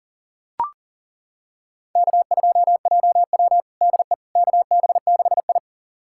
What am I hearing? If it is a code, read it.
Morse code, heard as a keyed tone, about 40 wpm, 700 Hz. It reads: K2JW DE KB6I
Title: USA Callsign DE Callsign - 4 Characters - 40wpm